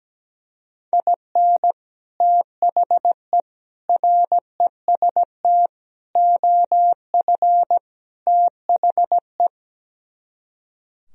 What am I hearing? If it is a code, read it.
Morse code, heard as a keyed tone, about 17 wpm, 700 Hz. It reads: IN THE REST OF THE
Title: Speed-Racing - Sets of 5 Words 17wpm